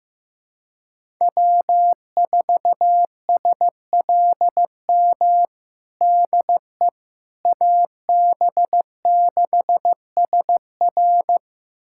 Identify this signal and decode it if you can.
Morse code, heard as a keyed tone, about 15 wpm, 700 Hz. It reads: W4SLM DE AB6SR
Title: USA Callsign DE Callsign - 5 Characters - 15wpm